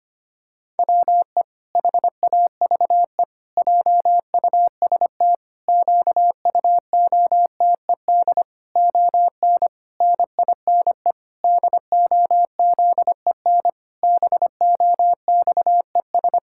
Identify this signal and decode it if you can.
Morse code, heard as a keyed tone, about 25 wpm, 700 Hz. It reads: WE HAVE JUST QUOTED ON NINE DOZEN BOXES
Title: Pangram Challenge v2 - 25wpm